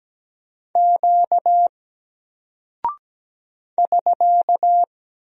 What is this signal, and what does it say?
Morse code, heard as a keyed tone, about 17 wpm, 700 Hz. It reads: Q <SK>
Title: Single Character (Letter:Number:Punctuation:Prosign) 17wpm